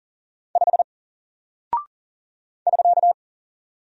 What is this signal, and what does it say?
Morse code, heard as a keyed tone, about 41 wpm, 700 Hz. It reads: 5 <SK>
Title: Single Character (Letter:Number:Punctuation:Prosign) 40wpm